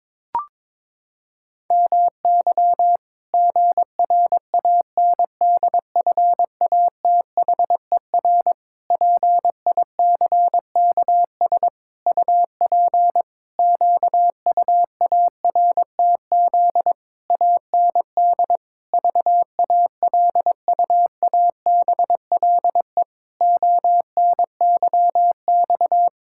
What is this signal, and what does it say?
Morse code, heard as a keyed tone, about 22 wpm, 700 Hz. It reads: MY GRANDFATHER PICKS UP QUARTZ AND VALUABLE ONYX
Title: Pangram Challenge v2 - 22wpm